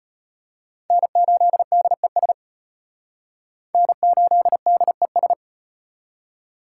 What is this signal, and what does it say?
Morse code, heard as a keyed tone, about 38 wpm, 700 Hz. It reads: N8DES N8DES